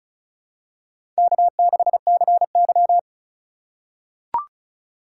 Morse code, heard as a keyed tone, about 36 wpm, 700 Hz. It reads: K6CY
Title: USA Call Signs - 4 Characters - 3x Character Spacing - 35wpm